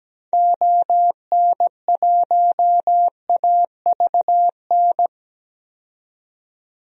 Morse code, heard as a keyed tone, about 17 wpm, 700 Hz. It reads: ON1AVN
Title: DXCC-100 Call Signs - 6 Characters - 17wpm